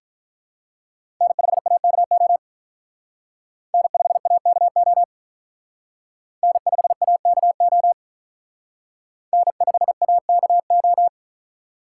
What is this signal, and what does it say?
Morse code, heard as a keyed tone, about 44 wpm, 700 Hz. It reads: N5AKO N5AKO N5AKO N5AKO